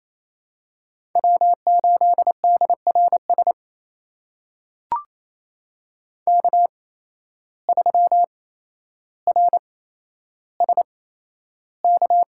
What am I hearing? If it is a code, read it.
Morse code, heard as a keyed tone, about 28 wpm, spaced as for 12 wpm, 700 Hz. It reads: W8DRS K3RSK
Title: USA Call Signs - 5 Characters - 8x Character Spacing - 28wpm